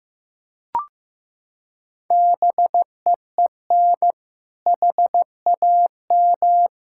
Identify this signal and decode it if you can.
Morse code, heard as a keyed tone, about 16 wpm, 700 Hz. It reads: BEEN HAM
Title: K1USN SST QSO Practice 15wpm